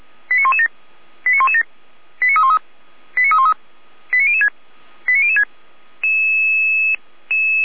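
Five-tone selective call (ZVEI), heard as quick runs of five tones, tones of about 70 ms. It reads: 89117 89117 89313 89313 89006 89006
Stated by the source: ZVEI five-tone selective call received on an SDR.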